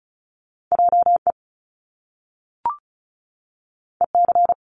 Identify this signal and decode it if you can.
Morse code, heard as a keyed tone, about 36 wpm, 700 Hz. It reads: JE EC
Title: Top Two Letter Combinations in Words 35wpm